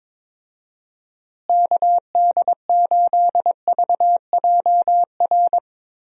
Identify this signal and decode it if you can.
Morse code, heard as a keyed tone, about 23 wpm, 700 Hz. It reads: KD8VJR
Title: USA Call Signs - 6 Characters - Repeated 2x - 22wpm